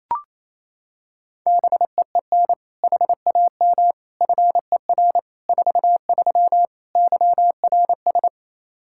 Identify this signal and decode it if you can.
Morse code, heard as a keyed tone, about 28 wpm, 700 Hz. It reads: BEEN HAM FER 43 YRS
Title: Practice QSO - Exchange 3 - More Info - 28wpm